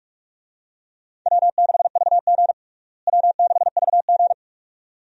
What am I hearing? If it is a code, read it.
Morse code, heard as a keyed tone, about 47 wpm, 700 Hz. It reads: W6VG W6VG
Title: USA Call Signs - 4 Characters - Repeated 2x - 45wpm